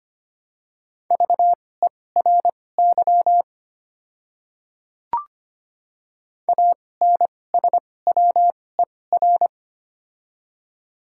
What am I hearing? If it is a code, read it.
Morse code, heard as a keyed tone, about 25 wpm, spaced as for 18 wpm, 700 Hz. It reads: VERY ANSWER
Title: 2x Character Spacing - Top 300 Words - 25wpm